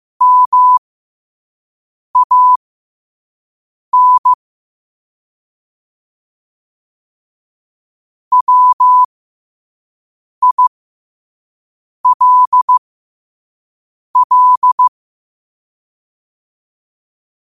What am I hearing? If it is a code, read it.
Morse code, heard as a keyed tone, about 15 wpm, spaced as for 5 wpm, 1000 Hz. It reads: MAN WILL
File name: qotdfiles / 5wpm